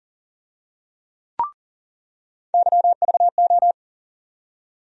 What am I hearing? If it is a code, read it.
Morse code, heard as a keyed tone, about 40 wpm, 700 Hz. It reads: YVO
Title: USA Call Sign Postfixes - 3 Letters 40wpm